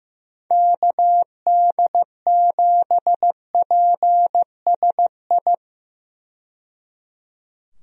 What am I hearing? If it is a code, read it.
Morse code, heard as a keyed tone, about 15 wpm, 700 Hz. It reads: KD7PSI
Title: USA Call Signs - 6 Characters - 15wpm